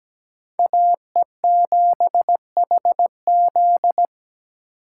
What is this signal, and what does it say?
Morse code, heard as a keyed tone, about 17 wpm, 700 Hz. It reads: AE7HZ
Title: USA Call Signs - 5 Characters - 8x Character Spacing - 17wpm